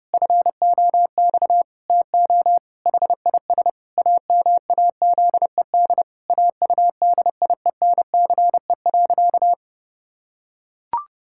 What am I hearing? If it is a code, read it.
Morse code, heard as a keyed tone, about 30 wpm, 700 Hz. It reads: FOX TO HIS AMAZED AUDIENCE.
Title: Pangram Challenge v2 - 30wpm